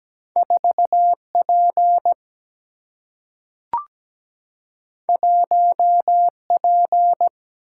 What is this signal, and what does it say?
Morse code, heard as a keyed tone, about 17 wpm, 700 Hz. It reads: 4 P 1 P
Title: Single-Letter Single-Number 17wpm